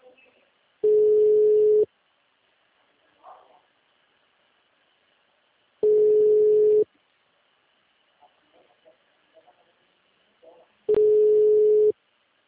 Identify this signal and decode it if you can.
Ringback tone (Europe, 425 Hz), with speech.
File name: out- -2028- - - .96898